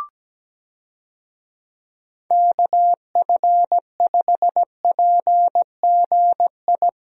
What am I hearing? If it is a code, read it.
Morse code, heard as a keyed tone, about 17 wpm, 700 Hz. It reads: KF5PGI
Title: USA Call Signs - 6 Characters - 17wpm